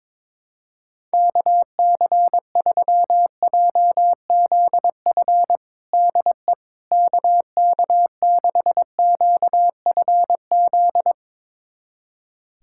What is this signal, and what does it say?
Morse code, heard as a keyed tone, about 22 wpm, 700 Hz. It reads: KC3JZF DE KK6QFZ
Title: USA Callsign DE Callsign 22wpm